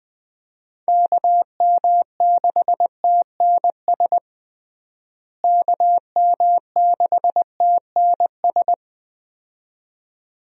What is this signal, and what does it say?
Morse code, heard as a keyed tone, about 21 wpm, 700 Hz. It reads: KM6TNS KM6TNS
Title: USA Call Signs - 6 Characters - Repeated 2x - 20wpm